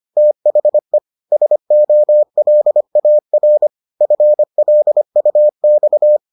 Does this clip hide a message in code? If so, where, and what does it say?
Yes: Morse code from 0.2 to 6.2 s, heard as a keyed tone, about 25 wpm, 600 Hz. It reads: THE SOLAR FLUX